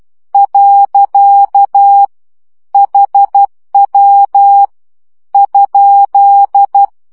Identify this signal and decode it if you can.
Morse code, heard as a keyed tone, about 12 wpm, 800 Hz. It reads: . HW ?
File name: morze vizsga 2011-1 10